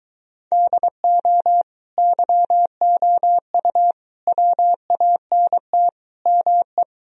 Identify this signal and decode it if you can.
Morse code, heard as a keyed tone, about 23 wpm, 700 Hz. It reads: DO YOU WANT ME